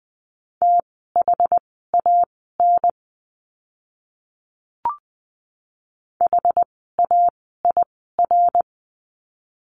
Morse code, heard as a keyed tone, about 20 wpm, spaced as for 14 wpm, 700 Hz. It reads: THAN HAIR
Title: 075 Lesson 13f - Letter L - Words 20wpm